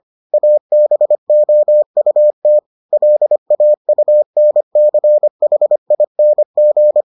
Morse code, heard as a keyed tone, about 25 wpm, 600 Hz. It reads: ABOUT LAUNCHING